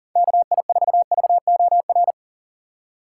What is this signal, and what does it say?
Morse code, heard as a keyed tone, about 40 wpm, 700 Hz. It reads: KI4VOR
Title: USA Call Signs - 6 Characters - 2x Character Spacing - 40wpm